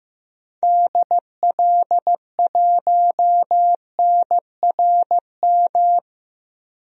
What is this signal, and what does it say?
Morse code, heard as a keyed tone, about 15 wpm, 700 Hz. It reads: DL1NRM
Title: DXCC-100 Call Signs - 6 Characters - Repeated 2x - 15wpm